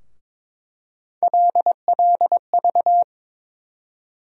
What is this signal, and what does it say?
Morse code, heard as a keyed tone, about 22 wpm, 700 Hz. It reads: LLV
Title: USA Call Sign Postfixes - 3 Letters 22wpm